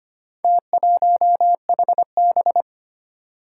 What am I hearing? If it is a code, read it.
Morse code, heard as a keyed tone, about 25 wpm, 700 Hz. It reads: T1HB
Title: Four Challenging Letter-Number Combinations 25wpm